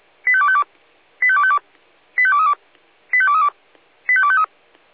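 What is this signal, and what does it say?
Five-tone selective call (ZVEI), heard as quick runs of five tones, tones of about 70 ms. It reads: 85252 85252 85232 85232 85253
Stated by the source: ZVEI five-tone selective call 85885 received by SDR.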